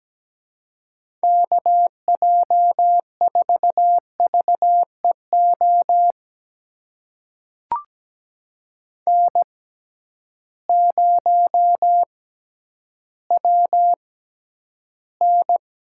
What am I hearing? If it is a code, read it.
Morse code, heard as a keyed tone, about 17 wpm, spaced as for 10 wpm, 700 Hz. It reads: KJ4VEO N0WN
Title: USA Call Signs - 6x Character Spacing - 17wpm